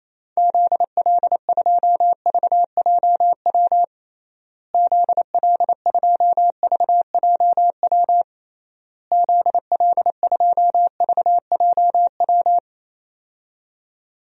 Morse code, heard as a keyed tone, about 28 wpm, 700 Hz. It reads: ZL2VJW ZL2VJW ZL2VJW
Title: DXCC-100 Call Signs - 6 Characters - Repeated 3x - 28wpm